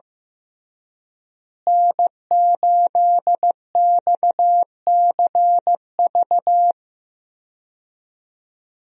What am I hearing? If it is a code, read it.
Morse code, heard as a keyed tone, about 15 wpm, 700 Hz. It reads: N8XCV
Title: USA Call Signs - 5 Characters - Repeated 3x - 15wpm